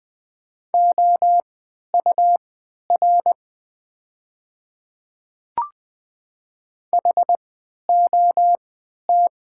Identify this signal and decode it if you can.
Morse code, heard as a keyed tone, about 20 wpm, spaced as for 11 wpm, 700 Hz. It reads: OUR HOT
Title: 3x Character Spacing - Top 300 Words - 20wpm